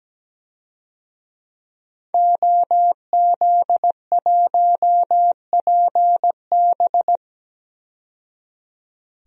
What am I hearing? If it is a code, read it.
Morse code, heard as a keyed tone, about 17 wpm, 700 Hz. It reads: OZ1PB
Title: DXCC-100 Call Signs - 5 Characters - 17wpm